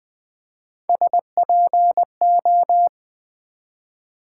Morse code, heard as a keyed tone, about 20 wpm, 700 Hz. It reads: SPO
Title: Build A Word - 1 Letter At A Time - 5-Letter Words - 20wpm